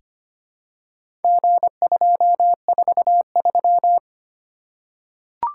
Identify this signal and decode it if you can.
Morse code, heard as a keyed tone, about 25 wpm, 700 Hz. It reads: G243
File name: practice / 25wpm / Four Letter-Number Combinations - 25wpm